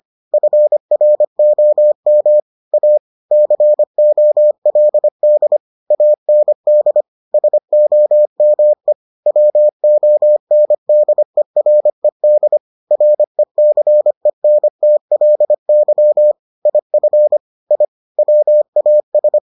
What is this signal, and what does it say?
Morse code, heard as a keyed tone, about 25 wpm, 600 Hz. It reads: FROM A COLD AND SOME WONDERED RECENTLY IF I WAS